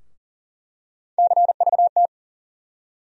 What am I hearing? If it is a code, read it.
Morse code, heard as a keyed tone, about 40 wpm, 700 Hz. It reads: CVT
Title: USA Call Sign Postfixes - 3 Letters 40wpm